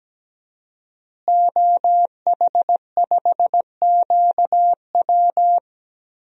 Morse code, heard as a keyed tone, about 17 wpm, 700 Hz. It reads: OH5QW
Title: DXCC-100 Call Signs - 5 Characters - Repeated 2x - 17wpm